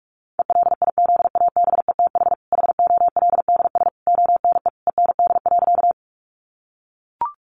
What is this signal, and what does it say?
Morse code, heard as a keyed tone, about 44 wpm, 700 Hz. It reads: ELIZABETH HOLDS ONE END.
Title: Easy Phrases 45wpm